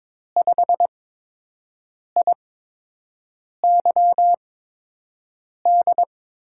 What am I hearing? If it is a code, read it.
Morse code, heard as a keyed tone, about 22 wpm, spaced as for 6 wpm, 700 Hz. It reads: 5IYD
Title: USA Call Signs - 6 Characters - 8x Character Spacing - 22wpm